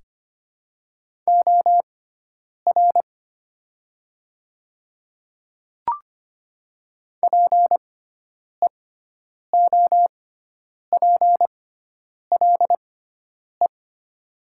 Morse code, heard as a keyed tone, about 25 wpm, spaced as for 9 wpm, 700 Hz. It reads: OR PEOPLE
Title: 6x Character Spacing - Top 100 Words - 25wpm